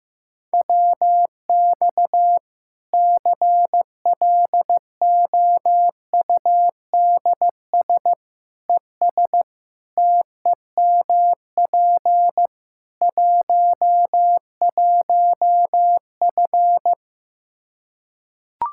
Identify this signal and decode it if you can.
Morse code, heard as a keyed tone, about 15 wpm, 700 Hz. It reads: WX CLOUDS ES TEMP 11F
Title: Practice QSO - Exchange 2 - Info - 15wpm